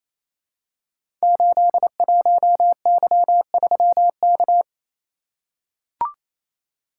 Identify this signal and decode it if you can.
Morse code, heard as a keyed tone, about 29 wpm, 700 Hz. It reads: 81Y3K
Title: Code Groups - 5 Letter:Numbers - 28wpm